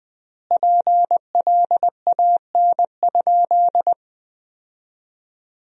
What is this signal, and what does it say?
Morse code, heard as a keyed tone, about 20 wpm, 700 Hz. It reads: PLAN?
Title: 4x Word-Spacing - Sentences from Top 500 Words - 20wpm